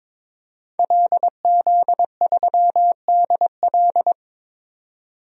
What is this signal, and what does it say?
Morse code, heard as a keyed tone, about 22 wpm, 700 Hz. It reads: LZ3DL
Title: DXCC-100 Call Signs - 5 Characters - 22wpm